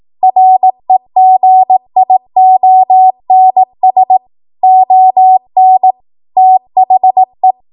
Morse code, heard as a keyed tone, about 18 wpm, 750 Hz. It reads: REGIONS ON THE
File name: Morse / 18WPM